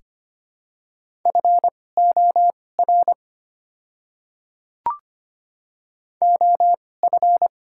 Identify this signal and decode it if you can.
Morse code, heard as a keyed tone, about 25 wpm, spaced as for 18 wpm, 700 Hz. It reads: FOR OF
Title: 2x Character Spacing - Top 300 Words - 25wpm